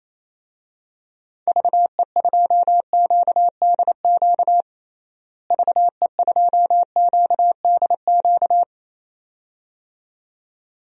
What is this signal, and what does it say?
Morse code, heard as a keyed tone, about 28 wpm, 700 Hz. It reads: VE2QDQ VE2QDQ
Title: DXCC-100 Call Signs - 6 Characters - Repeated 2x - 28wpm